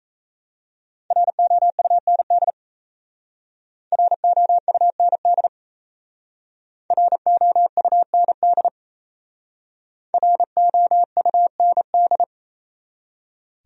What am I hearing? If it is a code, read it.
Morse code, heard as a keyed tone, about 34 wpm, 700 Hz. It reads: ROUND ROUND ROUND ROUND
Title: Speed-Racing Top 200 Words 28wpm